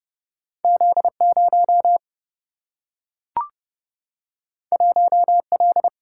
Morse code, heard as a keyed tone, about 30 wpm, 700 Hz. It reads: Z 0 1 L
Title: Single-Letter Single-Number 30wpm